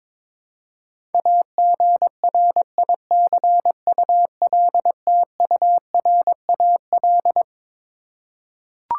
Morse code, heard as a keyed tone, about 21 wpm, 700 Hz. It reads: AGRICULTURAL
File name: Twelve Letter Words - 22wpm